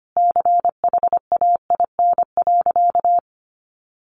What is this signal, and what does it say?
Morse code, heard as a keyed tone, about 25 wpm, 700 Hz. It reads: CHAIN.
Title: Sentences from Top 1500 Words 25wpm